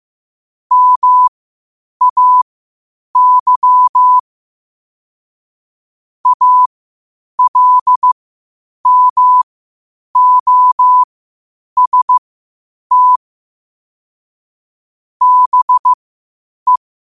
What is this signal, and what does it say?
Morse code, heard as a keyed tone, about 15 wpm, spaced as for 8 wpm, 1000 Hz. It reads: MAY ALMOST BE
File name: qotdfiles / 8wpm